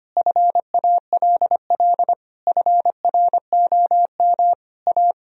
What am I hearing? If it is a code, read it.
Morse code, heard as a keyed tone, about 25 wpm, 700 Hz. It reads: FALL FROM A
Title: Sentences from Top 1000 Words - Review Part 2 - 25wpm